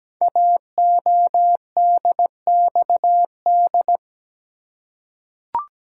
Morse code, heard as a keyed tone, about 17 wpm, 700 Hz. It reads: AODXD
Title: Code Groups - 5 Letters - 17wpm